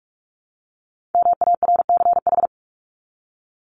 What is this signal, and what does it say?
Morse code, heard as a keyed tone, about 44 wpm, 700 Hz. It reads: MARCH
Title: Top 1501-1750 Words 45wpm